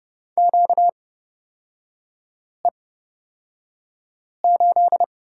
Morse code, heard as a keyed tone, about 29 wpm, spaced as for 5 wpm, 700 Hz. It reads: QE8
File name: Lesson 38e - Letter q - Three Characters 30wpm